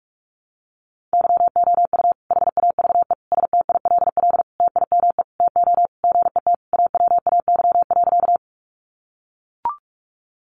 Morse code, heard as a keyed tone, about 44 wpm, 700 Hz. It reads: YOU HAVE STILL TIME TO GET AWAY.
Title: Sentences from Top 600 Words 45wpm